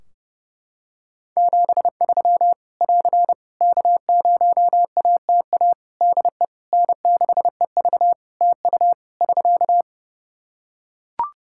Morse code, heard as a keyed tone, about 30 wpm, 700 Hz. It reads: 73 <AR> K0ATA DE N6EV TU <SK>
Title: Practice QSO - Exchange 4 - Ending - 30wpm